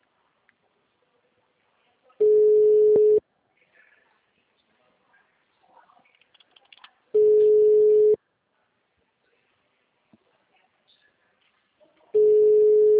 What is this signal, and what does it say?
Ringback tone (Europe, 425 Hz).